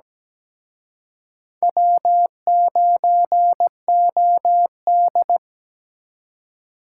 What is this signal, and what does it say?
Morse code, heard as a keyed tone, about 20 wpm, 700 Hz. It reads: W9OD
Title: Speed Racing - USA 1x2 Call Signs 17wpm